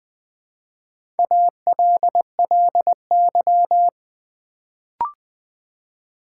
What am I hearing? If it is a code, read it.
Morse code, heard as a keyed tone, about 20 wpm, 700 Hz. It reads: ALLY
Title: Build A Word - 1 Letter At A Time - 4-Letter Words - 20wpm